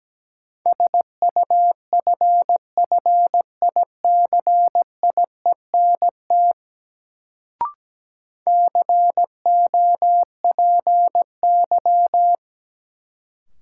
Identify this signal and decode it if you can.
Morse code, heard as a keyed tone, about 17 wpm, 700 Hz. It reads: SUFFICIENT COPY
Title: Sentences from Top 1400 Words - New Word Drill - 17wpm